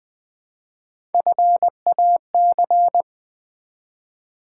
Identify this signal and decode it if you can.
Morse code, heard as a keyed tone, about 20 wpm, 700 Hz. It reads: FAC
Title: Build A Word - 1 Letter At A Time - 4-Letter Words - 20wpm